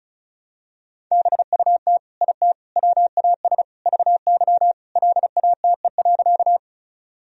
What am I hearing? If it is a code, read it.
Morse code, heard as a keyed tone, about 34 wpm, 700 Hz. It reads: BUT IT WAS VY LATE.
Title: Sentences from Top 300 Words and Abbreviations 35wpm